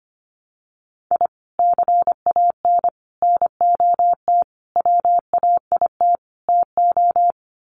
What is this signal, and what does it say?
Morse code, heard as a keyed tone, about 25 wpm, 700 Hz. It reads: I CAN NOT WAIT TO
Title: Sets of 5 Words Encore 25wpm